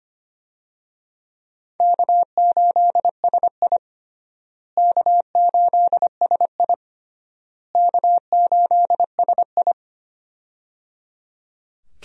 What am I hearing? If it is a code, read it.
Morse code, heard as a keyed tone, about 25 wpm, 700 Hz. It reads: K8SI K8SI K8SI
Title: USA Call Signs - 4 Characters - Repeated 3x - 25wpm